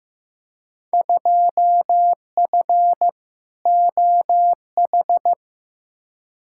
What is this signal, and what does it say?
Morse code, heard as a keyed tone, about 15 wpm, 700 Hz. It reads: 2F OH
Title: Field Day Exchanges - 15wpm